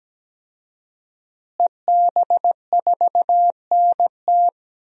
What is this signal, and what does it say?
Morse code, heard as a keyed tone, about 17 wpm, 700 Hz. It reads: EB4NT
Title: DXCC-100 Call Signs - 5 Characters - Repeated 2x - 17wpm